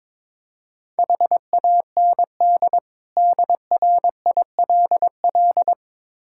Morse code, heard as a keyed tone, about 22 wpm, 700 Hz. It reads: HAND DRILL
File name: Easy Words - Speed-Racing with Hints - 22wpm